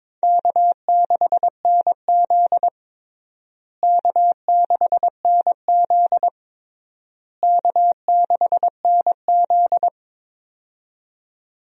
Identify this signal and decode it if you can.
Morse code, heard as a keyed tone, about 22 wpm, 700 Hz. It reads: K6NZ K6NZ K6NZ
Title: USA Call Signs - 4 Characters - Repeated 3x - 22wpm